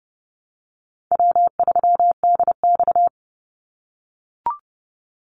Morse code, heard as a keyed tone, about 30 wpm, 700 Hz. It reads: W3DX
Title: NA SOTA Activator Callsigns 30wpm